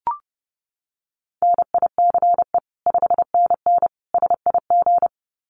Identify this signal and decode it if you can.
Morse code, heard as a keyed tone, about 30 wpm, 700 Hz. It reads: NICE 5NN SIG
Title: Common QSO Phrases 30wpm